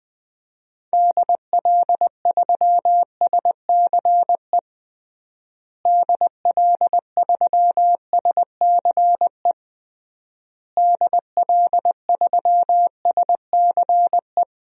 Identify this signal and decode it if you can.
Morse code, heard as a keyed tone, about 20 wpm, 700 Hz. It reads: DL3SCE DL3SCE DL3SCE
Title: DXCC-100 Call Signs - 6 Characters - Repeated 3x - 20wpm